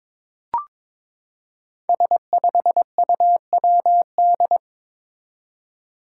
Morse code, heard as a keyed tone, about 21 wpm, 700 Hz. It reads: S5UWD
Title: Five Challenging Letter-Number Combinations 22wpm